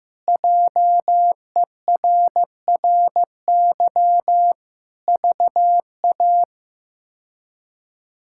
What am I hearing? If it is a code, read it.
Morse code, heard as a keyed tone, about 15 wpm, 700 Hz. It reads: JERRY VA
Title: K1USN SST QSO Practice - 15wpm